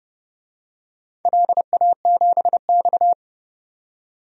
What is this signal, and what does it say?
Morse code, heard as a keyed tone, about 30 wpm, 700 Hz. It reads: LA7X
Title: DXCC-100 Call Signs - 4 Characters - 30wpm